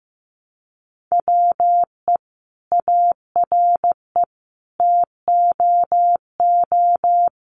Morse code, heard as a keyed tone, about 15 wpm, 700 Hz. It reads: WE ARE TOO
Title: Sentences from Top 1100 Words 15wpm